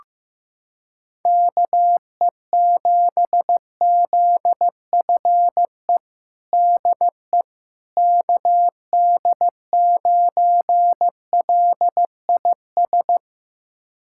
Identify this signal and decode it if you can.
Morse code, heard as a keyed tone, about 15 wpm, 700 Hz. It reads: KE7ZFE DE KD9LIS
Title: USA Callsign DE Callsign 15wpm